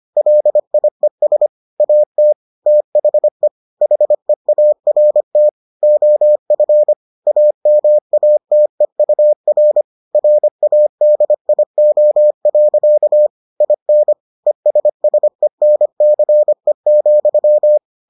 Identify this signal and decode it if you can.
Morse code, heard as a keyed tone, about 25 wpm, 600 Hz. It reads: LIES AT THE HEART OF AMATEUR RADIO. IN ESSENCE,